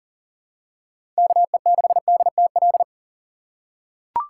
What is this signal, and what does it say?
Morse code, heard as a keyed tone, about 41 wpm, 700 Hz. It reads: KE6DTL
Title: USA Call Signs - 6 Characters - 4x Character Spacing - 40wpm